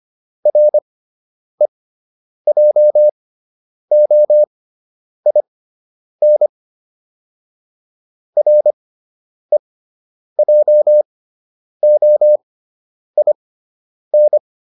Morse code, heard as a keyed tone, about 25 wpm, spaced as for 9 wpm, 600 Hz. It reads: REJOIN REJOIN